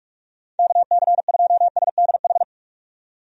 Morse code, heard as a keyed tone, about 43 wpm, 700 Hz. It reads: KC2SDH
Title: USA Call Signs - 6 Characters - 45wpm